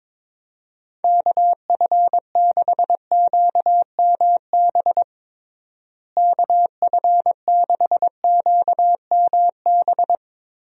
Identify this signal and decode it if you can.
Morse code, heard as a keyed tone, about 22 wpm, 700 Hz. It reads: KF6QMB KF6QMB
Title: USA Call Signs - 6 Characters - Repeated 3x - 22wpm